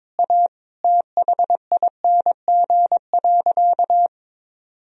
Morse code, heard as a keyed tone, about 22 wpm, 700 Hz. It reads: A THING.
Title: Sentences from Top 1000 Words - Review Part 1 - 22wpm